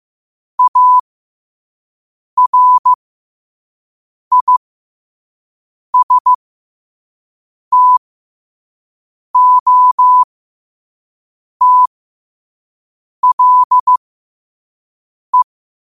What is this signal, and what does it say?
Morse code, heard as a keyed tone, about 15 wpm, spaced as for 5 wpm, 1000 Hz. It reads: ARISTOTLE
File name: qotdfiles / 5wpm